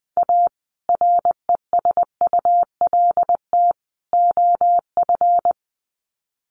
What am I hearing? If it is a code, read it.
Morse code, heard as a keyed tone, about 20 wpm, 700 Hz. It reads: A RESULT OF
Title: Sets of 3 Words 20wpm